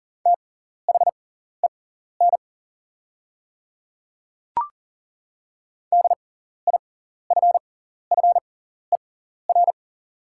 Morse code, heard as a keyed tone, about 37 wpm, spaced as for 14 wpm, 690 Hz. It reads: THEN DIFFER
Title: 6x Character Spacing - Top 300 Words - 40wpm